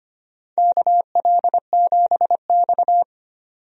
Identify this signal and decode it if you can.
Morse code, heard as a keyed tone, about 25 wpm, 700 Hz. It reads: KL7X
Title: DXCC-100 Call Signs - 4 Characters - 25wpm